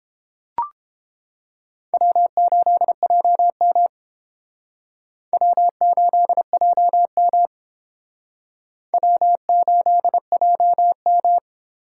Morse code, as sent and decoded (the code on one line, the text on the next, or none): .-- ---.. .--- -- / .-- ---.. .--- -- / .-- ---.. .--- --
W8JM W8JM W8JM